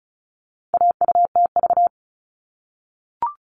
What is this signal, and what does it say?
Morse code, heard as a keyed tone, about 34 wpm, 700 Hz. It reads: AUTV
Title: All Cut Numbers - 4-Digit 35wpm